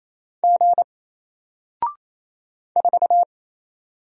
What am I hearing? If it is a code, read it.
Morse code, heard as a keyed tone, about 27 wpm, 700 Hz. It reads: G 4
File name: Single-Letter-Number-28wpm